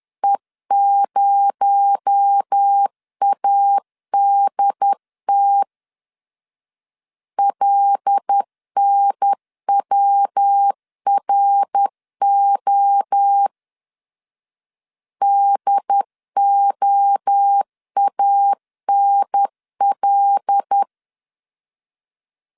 Morse code, heard as a keyed tone, about 10 wpm, 800 Hz. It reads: E0ADT LNWRO DOANL
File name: CW / DET / 1 6 1500C S B 10wpm 4137